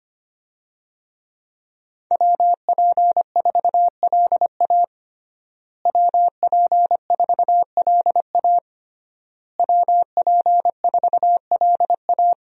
Morse code, heard as a keyed tone, about 25 wpm, 700 Hz. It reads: WP4LA WP4LA WP4LA
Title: USA Call Signs - 5 Characters - Repeated 3x - 25wpm